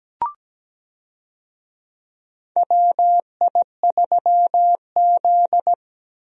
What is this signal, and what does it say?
Morse code, heard as a keyed tone, about 17 wpm, 700 Hz. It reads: WI3Z
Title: USA Call Signs - 4 Characters - Repeated 3x - 17wpm